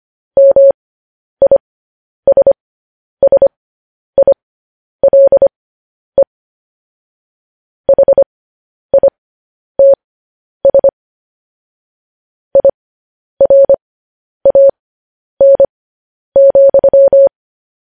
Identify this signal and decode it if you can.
Morse code, heard as a keyed tone, about 25 wpm, spaced as for 10 wpm, 550 Hz. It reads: MISSILE HITS IRAN,